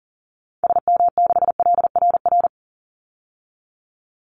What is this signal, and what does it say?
Morse code, heard as a keyed tone, about 40 wpm, 700 Hz. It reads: SM6LRR
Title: Call Signs Repeated 3x - 40wpm